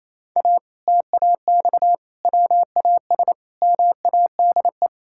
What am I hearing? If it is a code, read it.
Morse code, heard as a keyed tone, about 28 wpm, 700 Hz. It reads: A TAX WAS MADE
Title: Sentences from Top 1000 Words - Review Part 6 - 28wpm